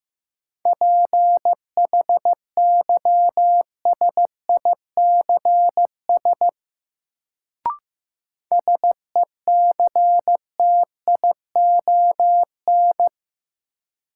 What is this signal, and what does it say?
Morse code, heard as a keyed tone, about 15 wpm, 700 Hz. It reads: PHYSICS SECTION
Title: Sentences from Top 1000 Words - Review Part 5 - New Word Drill - 15wpm